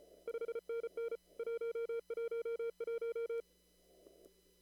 Morse code, heard as a keyed tone, about 34 wpm, 460 Hz. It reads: 5NN 111
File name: CW / kt5j wpx cw-op-k5pi